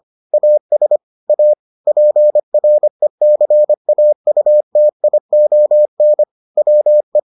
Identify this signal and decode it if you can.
Morse code, heard as a keyed tone, about 25 wpm, 600 Hz. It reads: AS A PRECAUTION WE